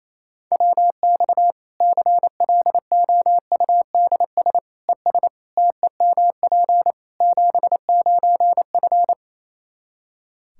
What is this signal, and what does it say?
Morse code, heard as a keyed tone, about 27 wpm, 700 Hz. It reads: WX CLOUDS ES TEMP 79F
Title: Practice QSO - Exchange 2 - Info - 28wpm